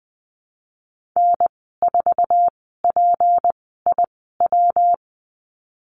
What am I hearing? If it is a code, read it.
Morse code, heard as a keyed tone, about 20 wpm, spaced as for 14 wpm, 700 Hz. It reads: N4PIW
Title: 193 Lesson 31g - Letter v - Call Signs 20wpm